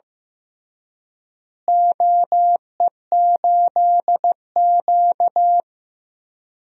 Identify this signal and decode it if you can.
Morse code, heard as a keyed tone, about 15 wpm, 700 Hz. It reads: OE8Q
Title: DXCC-100 Call Signs - 4 Characters - Repeated 3x - 15wpm